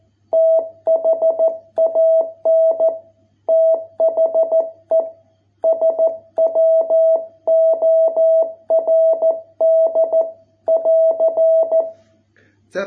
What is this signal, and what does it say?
Morse code, heard as a keyed tone, about 14 wpm, 660 Hz. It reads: THAN THE SWORD <AR>